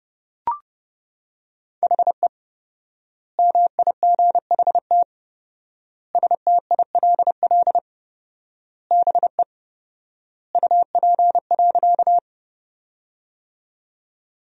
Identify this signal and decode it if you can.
Morse code, heard as a keyed tone, about 32 wpm, 700 Hz. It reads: HE MIGHT STILL BE UP.
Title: 4x Word-Spacing - Sentences from Top 300 Words - 30wpm